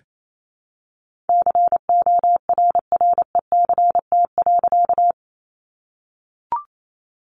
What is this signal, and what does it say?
Morse code, heard as a keyed tone, about 28 wpm, 700 Hz. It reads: CORRECT.
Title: A Sound Of Thunder 28wpm